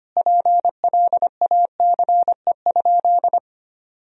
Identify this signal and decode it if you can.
Morse code, heard as a keyed tone, about 25 wpm, 700 Hz. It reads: PLACE?
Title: Sentences from Top 1000 Words - Review Part 4 - 25wpm